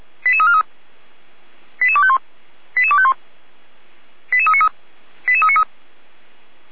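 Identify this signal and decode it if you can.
Five-tone selective call (ZVEI), heard as quick runs of five tones, tones of about 70 ms. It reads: 88343 88251 88251 88393 88393
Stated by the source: ZVEI five-tone selective call received by SDR.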